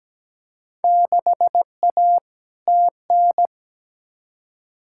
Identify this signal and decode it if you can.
Morse code, heard as a keyed tone, about 17 wpm, 700 Hz. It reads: 6A TN
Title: Field Day Exchanges Encore - 17wpm